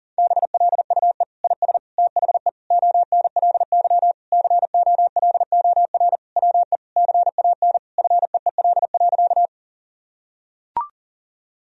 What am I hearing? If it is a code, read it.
Morse code, heard as a keyed tone, about 37 wpm, 700 Hz. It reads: BLUE IS THE ONLY COLOR WE CAN FEEL.
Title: Sentences from Top 1000 Words - Review Part 3 - 40wpm